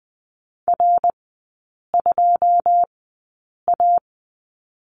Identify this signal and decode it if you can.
Morse code, heard as a keyed tone, about 20 wpm, spaced as for 8 wpm, 700 Hz. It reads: R2A
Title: 152 Lesson 25e - Letter p - Three Characters 20wpm